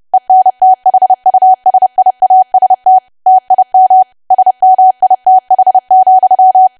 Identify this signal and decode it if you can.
Morse code, heard as a keyed tone, about 30 wpm, 750 Hz. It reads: ENTHUSIAST TIM SMITH,
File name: cwa / W1AW-Level3 / 30WPM